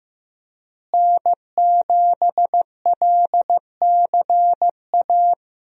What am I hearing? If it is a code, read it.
Morse code, heard as a keyed tone, about 16 wpm, 700 Hz. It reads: N7LCA
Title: USA Call Signs - 5 Characters - Repeated 2x - 15wpm